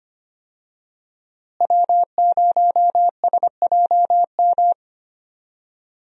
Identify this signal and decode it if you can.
Morse code, heard as a keyed tone, about 25 wpm, 700 Hz. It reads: W0SJM
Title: USA Call Signs - 5 Characters - Repeated 3x - 25wpm